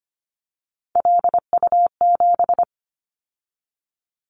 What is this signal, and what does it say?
Morse code, heard as a keyed tone, about 25 wpm, 700 Hz. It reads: LU7
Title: Callsign Prefixes for Top 50 DXCC - 25wpm